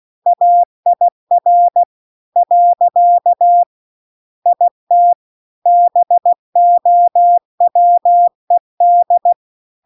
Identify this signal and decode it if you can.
Morse code, heard as a keyed tone, about 16 wpm, 700 Hz. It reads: AIR . IT BOWED
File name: CW-The-Big-Twoheaded-River-2-16WPM